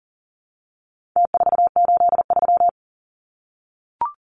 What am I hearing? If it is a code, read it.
Morse code, heard as a keyed tone, about 40 wpm, 700 Hz. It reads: T483
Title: Numbers - Contest Style - 4-Digit 40wpm